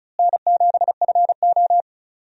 Morse code, heard as a keyed tone, about 36 wpm, 700 Hz. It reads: N7FO
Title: USA Call Signs - 4 Characters - Repeated 2x - 35wpm